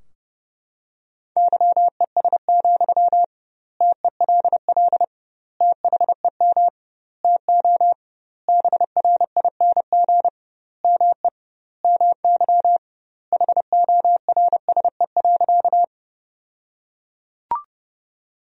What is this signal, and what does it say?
Morse code, heard as a keyed tone, about 32 wpm, 700 Hz. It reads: YES, TELL THEM TO BRING ME MY HORSE.
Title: 2x Word-Spacing - Sentences from Top 500 Words - 30wpm